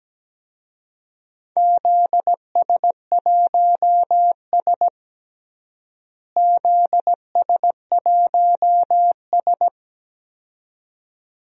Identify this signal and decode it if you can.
Morse code, heard as a keyed tone, about 17 wpm, 700 Hz. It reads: ZS1S ZS1S
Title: DXCC-100 Call Signs - 4 Characters - Repeated 2x - 17wpm